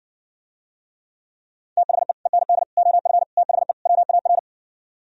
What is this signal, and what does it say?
Morse code, heard as a keyed tone, about 69 wpm, 700 Hz. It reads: THE END OF THE WAR